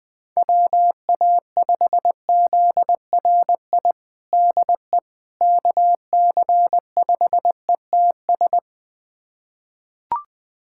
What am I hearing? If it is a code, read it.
Morse code, heard as a keyed tone, about 20 wpm, 700 Hz. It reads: WA5ZRI DE KC5ETS
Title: USA Callsign DE Callsign 20wpm